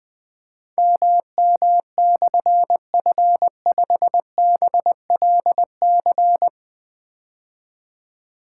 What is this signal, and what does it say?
Morse code, heard as a keyed tone, about 20 wpm, 700 Hz. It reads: MM/F5BLC
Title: Difficult Call Signs 20wpm